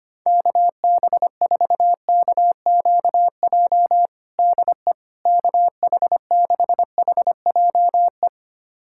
Morse code, heard as a keyed tone, about 25 wpm, 700 Hz. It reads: KB4KQJ DE KH6HJE
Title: USA Callsign DE Callsign 25wpm